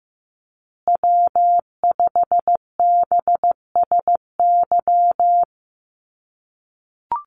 Morse code, heard as a keyed tone, about 15 wpm, 700 Hz. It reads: W5BSY
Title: USA Call Signs Repeated 2x 15wpm